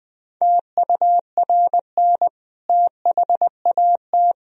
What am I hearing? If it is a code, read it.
Morse code, heard as a keyed tone, about 21 wpm, 700 Hz. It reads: TURN THAT
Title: Sentences from Top 1000 Words - Review Part 3 - 20wpm